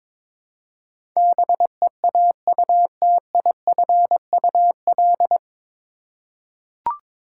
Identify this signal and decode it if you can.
Morse code, heard as a keyed tone, about 22 wpm, 700 Hz. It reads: BEAUTIFUL
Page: Morse Code Ninja practice 22wpm